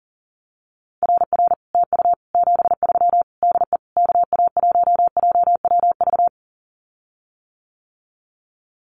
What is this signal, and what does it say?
Morse code, heard as a keyed tone, about 40 wpm, 700 Hz. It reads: RR TU 73 DE KA1JWV
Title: Common QSO Phrases Encore 40wpm